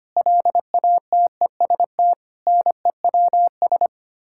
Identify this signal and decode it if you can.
Morse code, heard as a keyed tone, about 25 wpm, 700 Hz. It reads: LATEST NEWS
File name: Sets of 9 Words - 25wpm